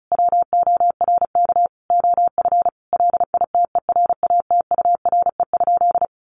Morse code, heard as a keyed tone, about 34 wpm, 700 Hz. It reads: WORK OF LITERATURE?
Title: Sentences from Top 600 Words 35wpm